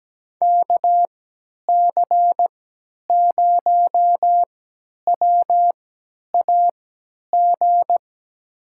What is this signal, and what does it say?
Morse code, heard as a keyed tone, about 17 wpm, spaced as for 10 wpm, 700 Hz. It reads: KC0WAG
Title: USA Call Signs - 6 Characters - 3x Character Spacing - 17wpm